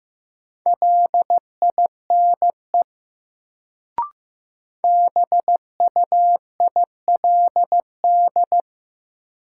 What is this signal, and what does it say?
Morse code, heard as a keyed tone, about 15 wpm, 700 Hz. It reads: LINE BUILD
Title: Sentences from Top 1000 Words - Review Part 1 - New Word Drill - 15wpm